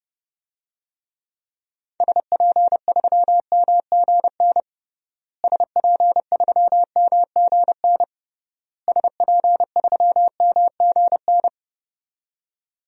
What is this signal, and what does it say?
Morse code, heard as a keyed tone, about 32 wpm, 700 Hz. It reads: SP3MGN SP3MGN SP3MGN
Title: DXCC-100 Call Signs - 6 Characters - Repeated 3x - 30wpm